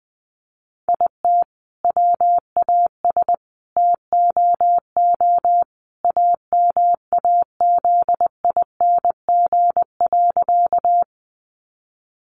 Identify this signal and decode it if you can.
Morse code, heard as a keyed tone, about 20 wpm, 700 Hz. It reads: IT WAS TOO AMAZING.
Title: Sentences from Top 1500 Words 20wpm